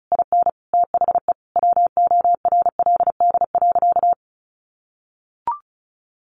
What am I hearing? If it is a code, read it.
Morse code, heard as a keyed tone, about 34 wpm, 700 Hz. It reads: IN THE WORLD.
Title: Sentences from Top 1100 Words 35wpm